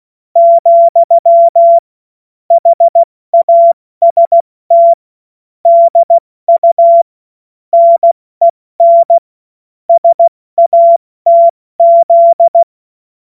Morse code, heard as a keyed tone, about 16 wpm, 680 Hz. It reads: , HAST DU NEN SATZ
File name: Morsekurs / Witze / Witz 06. Juli 2021, 70 F80